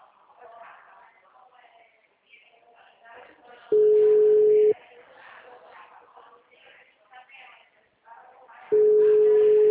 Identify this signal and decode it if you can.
Ringback tone (Europe, 425 Hz), with speech.